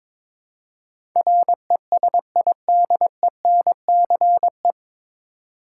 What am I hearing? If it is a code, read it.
Morse code, heard as a keyed tone, about 22 wpm, 700 Hz. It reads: RESIDENCE
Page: Morse Code Ninja practice 22wpm